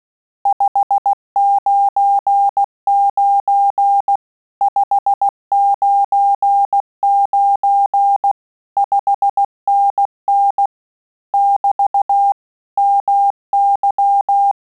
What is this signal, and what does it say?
Morse code, heard as a keyed tone, about 16 wpm, 800 Hz. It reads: 599 599 5NN <BT> MY
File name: Telegraphy-group / QSO CW 15 PPM